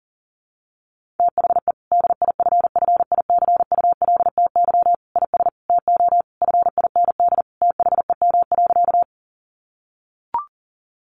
Morse code, heard as a keyed tone, about 40 wpm, 700 Hz. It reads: THE DIFFICULTY IS TO FIND THEM.
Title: Sentences from Top 1500 Words 40wpm